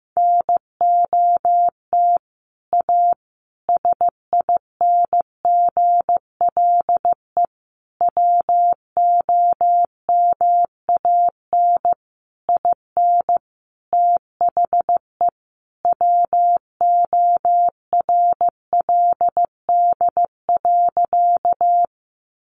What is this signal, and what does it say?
Morse code, heard as a keyed tone, about 15 wpm, 700 Hz. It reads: NOT A SINGLE WOMAN IN THE WORLD.
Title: Sentences from Top 1100 Words 15wpm